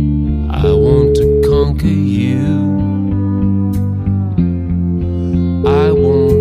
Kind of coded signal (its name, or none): telephone call progress tone